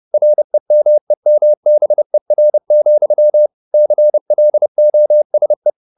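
Morse code, heard as a keyed tone, about 30 wpm, 600 Hz. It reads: REMEMBER, CLOSE
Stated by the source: Morse code at 30 wpm, the novel Never by Ken Follett in English, from CW med Gnister.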